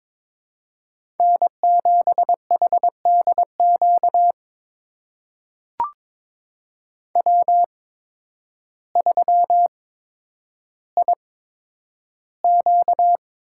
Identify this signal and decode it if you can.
Morse code, heard as a keyed tone, about 23 wpm, spaced as for 10 wpm, 700 Hz. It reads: N7HDQ W3IQ
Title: USA Call Signs - 8x Character Spacing - 22wpm